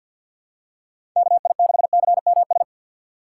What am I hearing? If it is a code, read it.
Morse code, heard as a keyed tone, about 53 wpm, 700 Hz. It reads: KI6CMS
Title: USA Call Signs - 6x Character Spacing - 50wpm